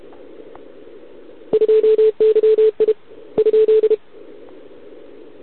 Morse code, heard as a keyed tone, about 31 wpm, 440 Hz. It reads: 2YI ?
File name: ARCHIV / OKDX2020 / OK1RIb